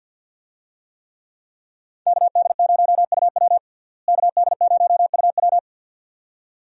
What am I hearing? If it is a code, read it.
Morse code, heard as a keyed tone, about 46 wpm, 700 Hz. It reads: KD0UW KD0UW
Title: USA Call Signs - 5 Characters - Repeated 2x - 50wpm